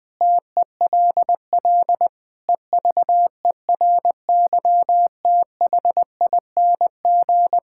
Morse code, heard as a keyed tone, about 20 wpm, 700 Hz. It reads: TELL EVERYTHING
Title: Sentences with 6-Word Phrases - 20wpm